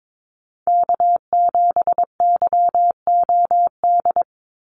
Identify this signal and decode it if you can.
Morse code, heard as a keyed tone, about 22 wpm, 700 Hz. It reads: K7YOD
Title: USA Call Signs 22wpm